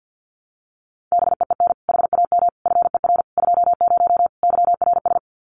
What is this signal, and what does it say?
Morse code, heard as a keyed tone, about 48 wpm, 700 Hz. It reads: BEEN HAM FER 20 YRS
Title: Common QSO Phrases 50wpm